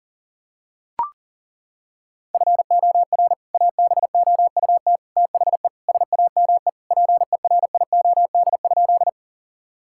Morse code, heard as a keyed tone, about 40 wpm, 700 Hz. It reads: FOR ABOUT THE SAME PERIOD?
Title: Sentences from Top 1000 Words - Review Part 5 - 40wpm